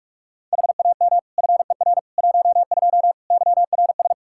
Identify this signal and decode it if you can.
Morse code, heard as a keyed tone, about 46 wpm, 700 Hz. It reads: HAM FER 12 YRS
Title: Practice QSO - Exchange 3 - More Info - 45wpm